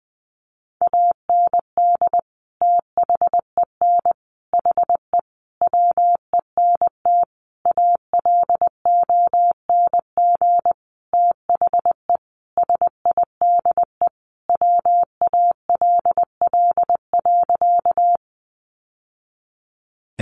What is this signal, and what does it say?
Morse code, heard as a keyed tone, about 20 wpm, 700 Hz. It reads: AND THEN HE WENT ALONG THE SIDE WALL.
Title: Sentences from Top 1100 Words 20wpm